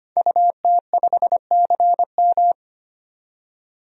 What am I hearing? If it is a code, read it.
Morse code, heard as a keyed tone, about 25 wpm, 700 Hz. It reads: UT5CM
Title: DXCC-100 Call Signs - 5 Characters - 25wpm